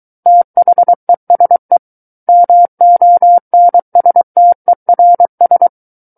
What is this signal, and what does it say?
Morse code, heard as a keyed tone, about 23 wpm, 700 Hz. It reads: THESE MONSTERS
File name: Book One / 14-Chapter 14 - 23 WPM